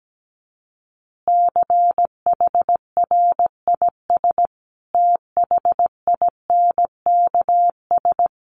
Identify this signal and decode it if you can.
Morse code, heard as a keyed tone, about 17 wpm, 700 Hz. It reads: CHRIS THINKS
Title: Easy Phrases 17wpm